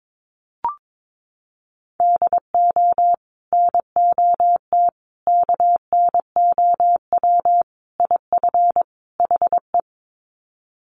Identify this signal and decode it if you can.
Morse code, heard as a keyed tone, about 22 wpm, 700 Hz. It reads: DO NOT KNOW IF HE
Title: Sets of 5 Words Encore2 22wpm